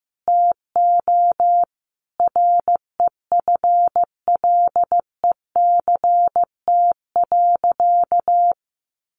Morse code, heard as a keyed tone, about 15 wpm, 700 Hz. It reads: TO REFLECT.
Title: Sentences from Top 1500 Words 15wpm